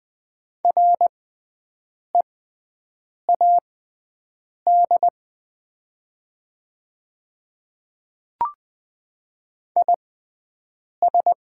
Morse code, heard as a keyed tone, about 20 wpm, spaced as for 7 wpm, 700 Hz. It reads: READ IS
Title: 6x Character Spacing - Top 300 Words - 20wpm